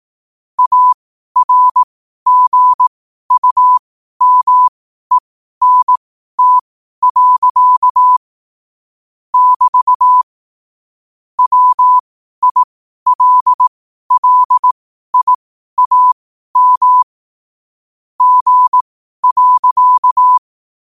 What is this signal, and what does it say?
Morse code, heard as a keyed tone, about 18 wpm, spaced as for 12 wpm, 1000 Hz. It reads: ARGUMENT. <BT> WILLIAM G.
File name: qotdfiles / 12wpm